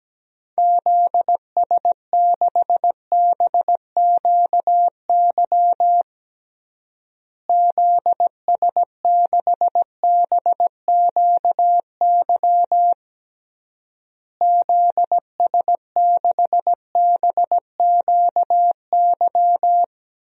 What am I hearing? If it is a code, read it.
Morse code, heard as a keyed tone, about 17 wpm, 700 Hz. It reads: ZS6BQY ZS6BQY ZS6BQY
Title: DXCC-100 Call Signs - 6 Characters - Repeated 3x - 17wpm